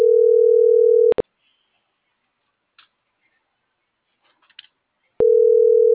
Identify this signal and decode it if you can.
Ringback tone (North America), 440 + 480 Hz.